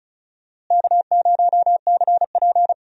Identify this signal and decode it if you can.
Morse code, heard as a keyed tone, about 36 wpm, 700 Hz. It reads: K0CP
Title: USA Call Signs - 4 Characters - Repeated 3x - 35wpm